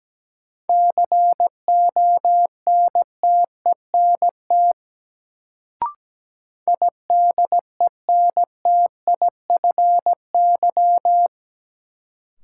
Morse code, heard as a keyed tone, about 17 wpm, 700 Hz. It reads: CONTENT IDENTIFY
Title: Sentences from Top 900 Words - New Word Drill - 17wpm